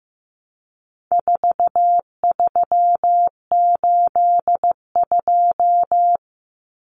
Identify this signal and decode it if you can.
Morse code, heard as a keyed tone, about 15 wpm, 700 Hz. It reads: 4382
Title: Common Cut Numbers - 4-Digit 15wpm